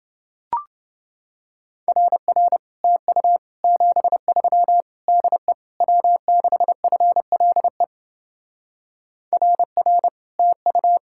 Morse code, heard as a keyed tone, about 28 wpm, 700 Hz. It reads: RR TU 73 DE W6FLE RR TU